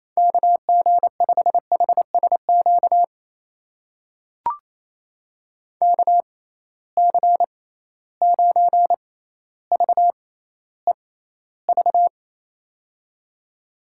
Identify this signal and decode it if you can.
Morse code, heard as a keyed tone, about 28 wpm, spaced as for 14 wpm, 700 Hz. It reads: KG5HSQ KC9VEV
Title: USA Call Signs - 6 Characters - 6x Character Spacing - 28wpm